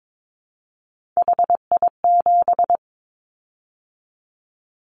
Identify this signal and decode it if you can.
Morse code, heard as a keyed tone, about 22 wpm, 700 Hz. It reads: HI7
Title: Callsign Prefixes for Top 100 DXCC - 22wpm